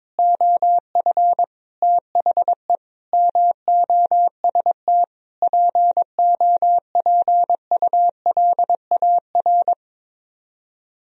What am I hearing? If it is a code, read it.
Morse code, heard as a keyed tone, about 22 wpm, 700 Hz. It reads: OF THE MOST POPULAR
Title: Speed-Racing - Sets of 5 Words 22wpm